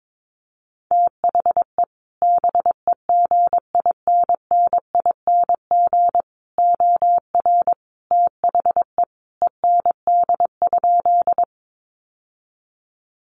Morse code, heard as a keyed tone, about 22 wpm, 700 Hz. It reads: THE BEGINNING OR THE END?
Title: Sentences from Top 800 Words 22wpm